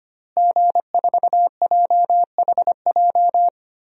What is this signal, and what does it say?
Morse code, heard as a keyed tone, about 25 wpm, 700 Hz. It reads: G4JHJ
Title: DXCC-100 Call Signs - 5 Characters - Repeated 3x - 25wpm